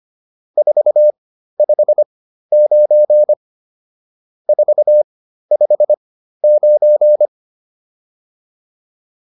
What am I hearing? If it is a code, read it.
Morse code, heard as a keyed tone, about 25 wpm, spaced as for 13 wpm, 600 Hz. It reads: 459 459